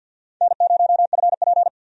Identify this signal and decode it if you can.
Morse code, heard as a keyed tone, about 49 wpm, 700 Hz. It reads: N0FP
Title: USA Call Signs - 4 Characters - 50wpm